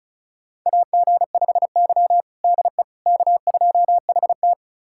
Morse code, heard as a keyed tone, about 36 wpm, 700 Hz. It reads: AG5Y DE K2HT
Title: USA Callsign DE Callsign - 4 Characters - 35wpm